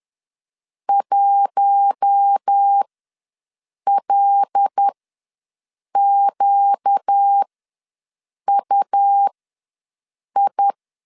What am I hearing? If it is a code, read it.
Morse code, heard as a keyed tone, about 10 wpm, spaced as for 6 wpm, 800 Hz. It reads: 1LQUI
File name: LEZIONI / 5 LBVC / 5 L ws 10wpm 3636